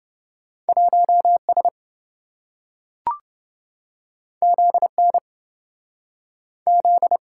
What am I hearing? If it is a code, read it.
Morse code, heard as a keyed tone, about 28 wpm, 700 Hz. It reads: 1S ZN Z